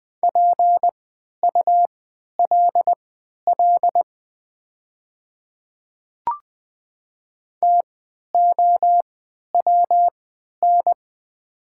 Morse code, heard as a keyed tone, about 21 wpm, spaced as for 12 wpm, 690 Hz. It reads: PULL TOWN
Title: 3x Character Spacing - Top 500 Words - 20wpm